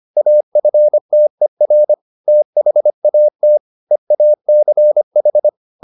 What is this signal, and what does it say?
Morse code, heard as a keyed tone, about 25 wpm, 600 Hz. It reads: AFTER THAT EACH